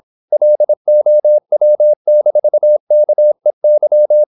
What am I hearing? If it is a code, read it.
Morse code, heard as a keyed tone, about 26 wpm, 600 Hz. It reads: LOW-KEY
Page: Never 26wpm | CW med Gnister